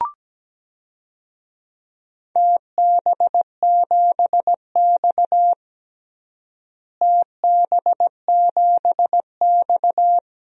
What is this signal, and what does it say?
Morse code, heard as a keyed tone, about 17 wpm, 700 Hz. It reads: TB7X TB7X
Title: DXCC-100 Call Signs - 4 Characters - Repeated 2x - 17wpm